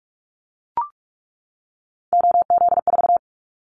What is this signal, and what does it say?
Morse code, heard as a keyed tone, about 44 wpm, 700 Hz. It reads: OZ4
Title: Callsign Prefixes for Top 100 DXCC - 45wpm